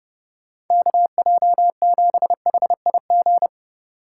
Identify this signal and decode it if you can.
Morse code, heard as a keyed tone, about 30 wpm, 700 Hz. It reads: KJ7HIG
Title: USA Call Signs - 6 Characters - 2x Character Spacing - 30wpm